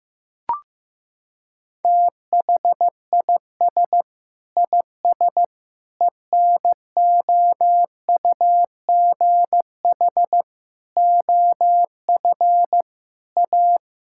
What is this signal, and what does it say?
Morse code, heard as a keyed tone, about 16 wpm, 700 Hz. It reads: THIS IS ENOUGH OF A
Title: Sentences from Top 1000 Words - Review Part 5 - 15wpm